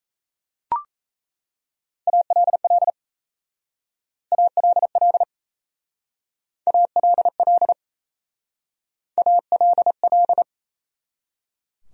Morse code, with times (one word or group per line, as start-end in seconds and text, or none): 2.1-2.9 ALL
4.3-5.2 ALL
6.7-7.7 ALL
9.2-10.4 ALL